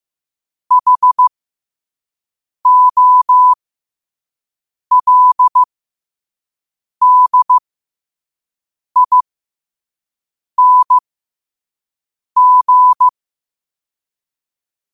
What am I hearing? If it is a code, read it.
Morse code, heard as a keyed tone, about 15 wpm, spaced as for 5 wpm, 1000 Hz. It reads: HOLDING
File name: qotdfiles / 5wpm